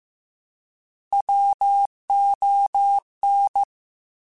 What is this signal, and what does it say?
Morse code, heard as a keyed tone, about 15 wpm, 800 Hz. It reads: WON
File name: Alphabetical / Alphabetical mp3s / Ws @13